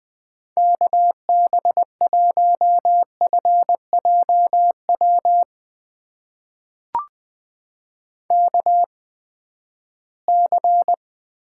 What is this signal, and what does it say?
Morse code, heard as a keyed tone, about 20 wpm, spaced as for 14 wpm, 700 Hz. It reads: KB1FJW KC
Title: USA Call Signs - 6 Characters - 8x Character Spacing - 20wpm